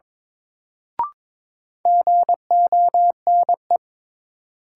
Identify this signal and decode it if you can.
Morse code, heard as a keyed tone, about 22 wpm, 700 Hz. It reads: GONE
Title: Sentences from Top 1200 Words - New Word Drill Alt - 22wpm